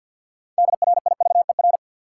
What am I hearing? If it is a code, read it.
Morse code, heard as a keyed tone, about 52 wpm, 700 Hz. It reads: DRIVER
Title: Sentences from Top 1000 Words - New Word Drill - 50wpm